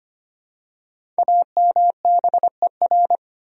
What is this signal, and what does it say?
Morse code, heard as a keyed tone, about 25 wpm, 700 Hz. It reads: AMBER
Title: USA Common Names - Max 5 Letters - 25wpm